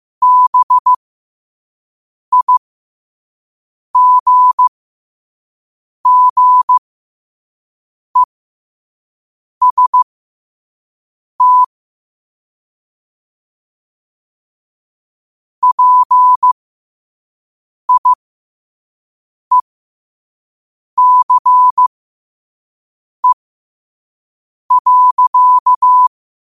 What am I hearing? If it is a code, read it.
Morse code, heard as a keyed tone, about 15 wpm, spaced as for 5 wpm, 1000 Hz. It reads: BIGGEST PIECE.
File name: qotdfiles / 5wpm